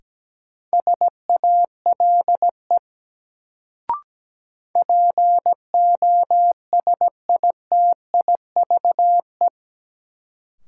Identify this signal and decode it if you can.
Morse code, heard as a keyed tone, about 17 wpm, 700 Hz. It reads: SALE POSITIVE
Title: Sentences from Top 1400 Words - New Word Drill Alt - 17wpm